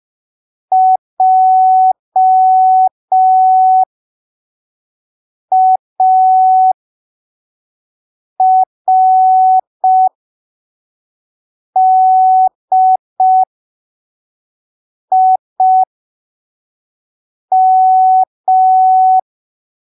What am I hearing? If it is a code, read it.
Morse code, heard as a keyed tone, about 5 wpm, spaced as for 3 wpm, 750 Hz. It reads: JARDIM
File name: arquivos / texto / texto 20 05ppm